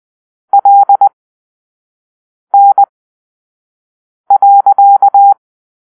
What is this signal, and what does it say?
Morse code, heard as a keyed tone, about 20 wpm, spaced as for 5 wpm, 800 Hz. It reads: LN.
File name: arrl / Grp-6 Drill-3